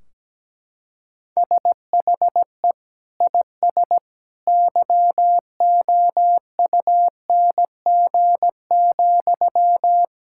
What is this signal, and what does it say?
Morse code, heard as a keyed tone, about 17 wpm, 700 Hz. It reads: SHE IS YOUNG,
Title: Sentences from Top 1600 Words 17wpm